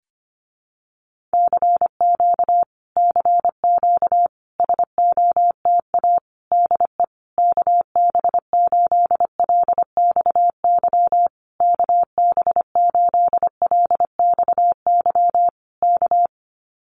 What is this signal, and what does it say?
Morse code, heard as a keyed tone, about 25 wpm, 700 Hz. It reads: CQ CQ SOTA DE KB8LXY KB8LXY K
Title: Common QSO Phrases Encore 25wpm